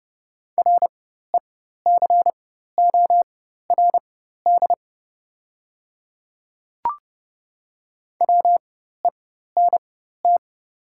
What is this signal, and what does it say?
Morse code, heard as a keyed tone, about 31 wpm, spaced as for 14 wpm, 700 Hz. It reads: RECORD WENT
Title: 4x Character Spacing - Top 500 Words - 30wpm